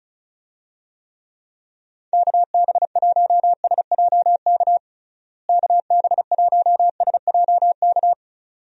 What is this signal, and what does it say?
Morse code, heard as a keyed tone, about 34 wpm, 700 Hz. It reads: KB1SJK KB1SJK
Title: USA Call Signs - 6 Characters - Repeated 2x - 35wpm